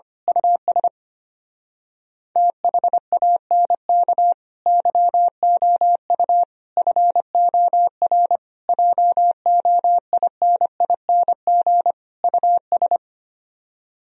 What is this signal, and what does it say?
Morse code, heard as a keyed tone, about 24 wpm, 700 Hz. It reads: US THANK YOU FOR JOINING US
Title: Speed-Racing - Sets of 5 Words 25wpm